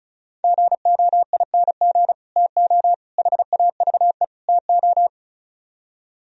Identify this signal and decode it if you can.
Morse code, heard as a keyed tone, about 43 wpm, 700 Hz. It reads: GOING TO HAVE TO
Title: Speed-Racing - Sets of 4 Words 35wpm